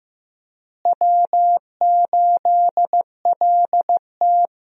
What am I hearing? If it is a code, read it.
Morse code, heard as a keyed tone, about 15 wpm, 700 Hz. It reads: W8LT
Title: USA Call Signs - 4 Characters - Repeated 3x - 15wpm